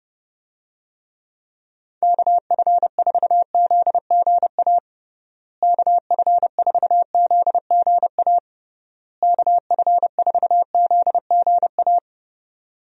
Morse code, heard as a keyed tone, about 30 wpm, 700 Hz. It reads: KF4ZGA KF4ZGA KF4ZGA
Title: USA Call Signs - 6 Characters - Repeated 3x - 30wpm